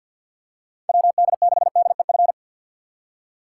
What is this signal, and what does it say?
Morse code, heard as a keyed tone, about 47 wpm, 700 Hz. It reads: WD6DEF
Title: USA Call Signs - 6 Characters - 4x Character Spacing - 50wpm